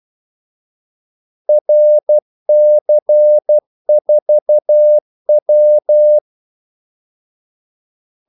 Morse code, heard as a keyed tone, about 12 wpm, 600 Hz. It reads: RC4W
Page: Kaldesignaler 12 wpm